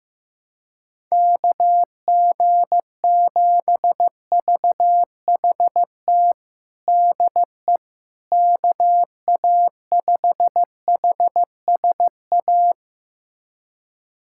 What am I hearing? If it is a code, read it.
Morse code, heard as a keyed tone, about 15 wpm, 700 Hz. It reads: KG7VHT DE KA5HSA
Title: USA Callsign DE Callsign - 6 Characters - 15wpm